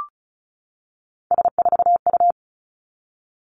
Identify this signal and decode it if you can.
Morse code, heard as a keyed tone, about 36 wpm, 700 Hz. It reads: S4U
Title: Three Challenging Letter-Number Combinations 35wpm